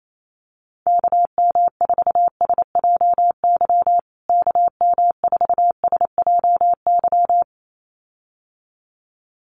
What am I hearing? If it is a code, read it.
Morse code, heard as a keyed tone, about 28 wpm, 700 Hz. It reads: KM4SJY KM4SJY
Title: USA Call Signs Repeated 2x 28wpm